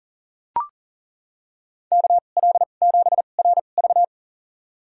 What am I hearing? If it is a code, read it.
Morse code, heard as a keyed tone, about 40 wpm, spaced as for 29 wpm, 700 Hz. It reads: KL7RV
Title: USA Call Signs - 5 Characters - 2x Character Spacing - 40wpm